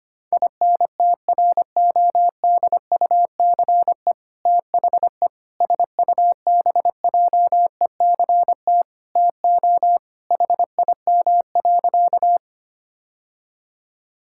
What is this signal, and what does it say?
Morse code, heard as a keyed tone, about 25 wpm, 700 Hz. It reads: INTRODUCE THE SUBJECT TO HIM.
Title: Sentences from Top 1600 Words 25wpm